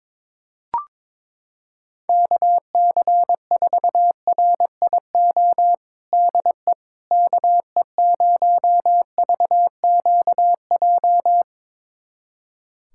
Morse code, heard as a keyed tone, about 22 wpm, 700 Hz. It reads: KC4RIO DE KE0VQJ
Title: USA Callsign DE Callsign 22wpm